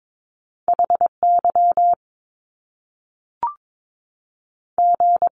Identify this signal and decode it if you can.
Morse code, heard as a keyed tone, about 22 wpm, 700 Hz. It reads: HY G
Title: Top Two Letter Combinations in Words 22wpm